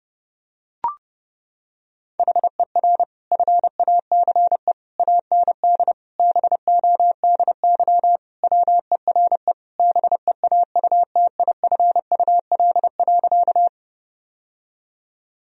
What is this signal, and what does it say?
Morse code, heard as a keyed tone, about 30 wpm, 700 Hz. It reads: HER FACE AND BODY WERE BEAUTIFUL.
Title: Sentences from Top 1000 Words - Review Part 2 - 30wpm